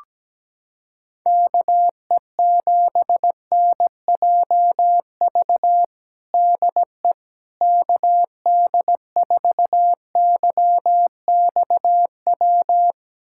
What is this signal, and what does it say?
Morse code, heard as a keyed tone, about 17 wpm, 700 Hz. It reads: KE7NJV DE KD4YXW
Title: USA Callsign DE Callsign 17wpm